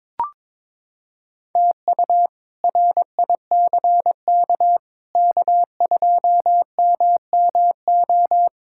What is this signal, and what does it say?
Morse code, heard as a keyed tone, about 22 wpm, 700 Hz. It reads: TU RICK K2MMO
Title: K1USN SST QSO Practice - 22wpm